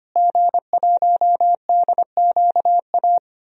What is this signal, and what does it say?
Morse code, heard as a keyed tone, about 25 wpm, 700 Hz. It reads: G1DQA
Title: DXCC-100 Call Signs - 5 Characters - Repeated 2x - 25wpm